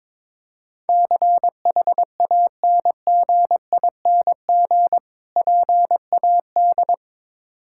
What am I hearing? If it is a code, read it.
Morse code, heard as a keyed tone, about 22 wpm, 700 Hz. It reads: CHANGING PAD
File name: Difficult Words with Hints - 22wpm